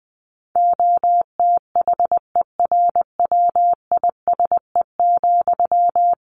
Morse code, heard as a keyed tone, about 19 wpm, 700 Hz. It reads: OTHERWISE,
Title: Sentences from Top 1200 Words 20wpm